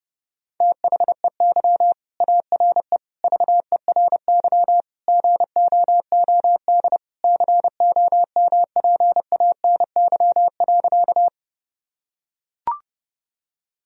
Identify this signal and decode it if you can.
Morse code, heard as a keyed tone, about 32 wpm, 700 Hz. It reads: THEY ARE VERY GOOD COMPANY.
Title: Sentences from Top 1000 Words - Review Part 3 - 30wpm